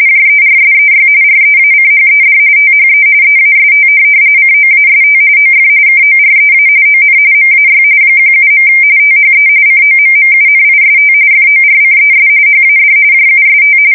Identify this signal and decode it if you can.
RTTY, about 45.45 baud. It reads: JERSEY REACHED THE EAST COAST OF KOREA ON MAY 17. THE COMMUNIST FORCES HAD JUST
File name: fsk170 / USS-New-Jersey